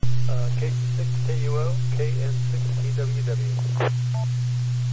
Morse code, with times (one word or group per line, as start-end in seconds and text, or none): none